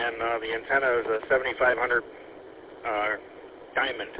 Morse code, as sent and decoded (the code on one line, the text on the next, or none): none